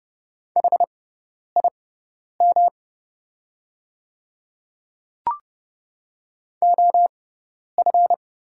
Morse code, heard as a keyed tone, about 30 wpm, spaced as for 10 wpm, 700 Hz. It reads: HIM OF
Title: 6x Character Spacing - Top 100 Words - 30wpm